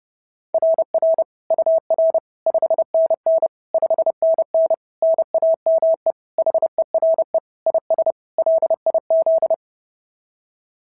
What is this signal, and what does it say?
Morse code, heard as a keyed tone, about 30 wpm, 690 Hz. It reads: RR UR 5NN 5NN NAME HERE IS LIZ
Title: SOTA QSO Practice 30wpm